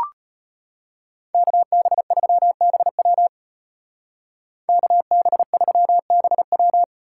Morse code, heard as a keyed tone, about 32 wpm, 700 Hz. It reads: KB3BW KB3BW